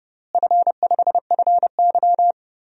Morse code, heard as a keyed tone, about 30 wpm, 700 Hz. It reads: F5FY
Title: DXCC-100 Call Signs - 4 Characters - Repeated 2x - 30wpm